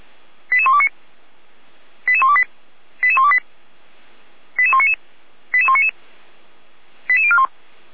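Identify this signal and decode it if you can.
Five-tone selective call (ZVEI), heard as quick runs of five tones, tones of about 70 ms. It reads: 88129 88128 88128 88199 88199 88041
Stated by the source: ZVEI five-tone call 88125 received by SDR, channel 455.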